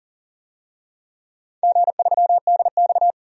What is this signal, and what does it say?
Morse code, heard as a keyed tone, about 40 wpm, 700 Hz. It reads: G3DX
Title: DXCC-100 Call Signs - 4 Characters - 40wpm